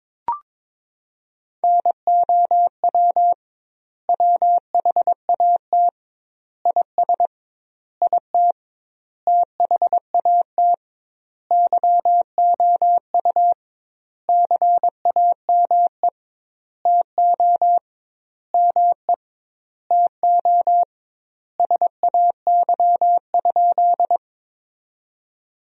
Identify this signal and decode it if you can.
Morse code, heard as a keyed tone, about 22 wpm, 700 Hz. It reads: NOW WHAT IS IT THAT YOU CAME TO ME TO SAY?
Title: 2x Word-Spacing - Sentences from Top 200 Words - 22wpm